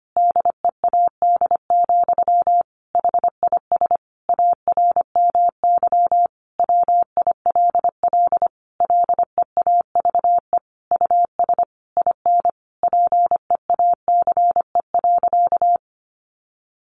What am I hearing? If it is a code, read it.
Morse code, heard as a keyed tone, about 25 wpm, 700 Hz. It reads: DEAD, HIS ARMY WILL LEAVE US IN PEACE.
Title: Sentences from Top 1500 Words 25wpm